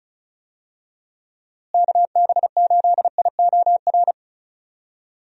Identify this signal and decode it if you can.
Morse code, heard as a keyed tone, about 36 wpm, 700 Hz. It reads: KB8IOR
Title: USA Call Signs - 6 Characters - Repeated 3x - 35wpm